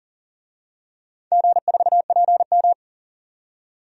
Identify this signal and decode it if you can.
Morse code, heard as a keyed tone, about 40 wpm, 700 Hz. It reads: G4PM
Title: DXCC-100 Call Signs - 4 Characters - 40wpm